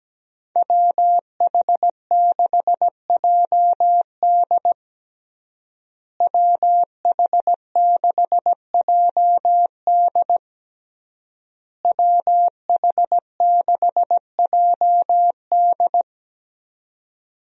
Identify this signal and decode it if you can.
Morse code, heard as a keyed tone, about 17 wpm, 700 Hz. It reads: WH6JD WH6JD WH6JD
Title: USA Call Signs - 5 Characters - Repeated 3x - 17wpm